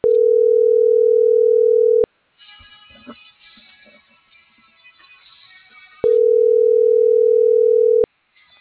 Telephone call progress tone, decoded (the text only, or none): ringback tone